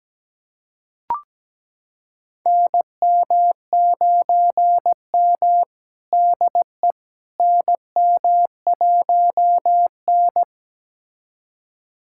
Morse code, heard as a keyed tone, about 17 wpm, 700 Hz. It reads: NM9M DE NM1N
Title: USA Callsign DE Callsign - 4 Characters - 17wpm